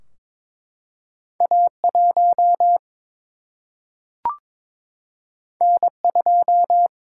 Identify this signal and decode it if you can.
Morse code, heard as a keyed tone, about 22 wpm, 700 Hz. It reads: A1 N2
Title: Single-Letter Single-Number 22wpm